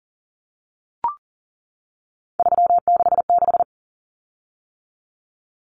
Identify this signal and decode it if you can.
Morse code, heard as a keyed tone, about 40 wpm, 700 Hz. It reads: 366
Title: Common Cut Numbers - 3-Digit 40wpm